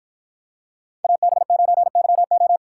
Morse code, heard as a keyed tone, about 50 wpm, 700 Hz. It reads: AB9YO